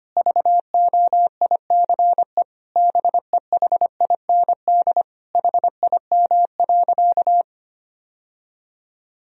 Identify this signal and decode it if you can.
Morse code, heard as a keyed tone, about 25 wpm, 700 Hz. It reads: VOICE BEHIND HIM.
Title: Sentences from Top 1000 Words - Review Part 1 - 25wpm